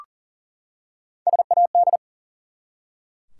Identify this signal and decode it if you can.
Morse code, heard as a keyed tone, about 40 wpm, 700 Hz. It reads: SAD
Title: Three Letter Words - 40wpm